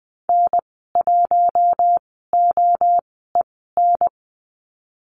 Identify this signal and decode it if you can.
Morse code, heard as a keyed tone, about 20 wpm, spaced as for 14 wpm, 700 Hz. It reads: N1OEN
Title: 042 Lesson 8g - Number 1 - Call Signs 20wpm